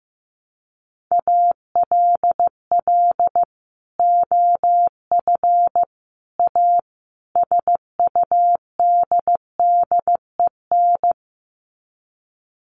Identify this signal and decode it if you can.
Morse code, heard as a keyed tone, about 15 wpm, 700 Hz. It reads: ALL OF A SUDDEN
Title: Sets of 4 Words 15wpm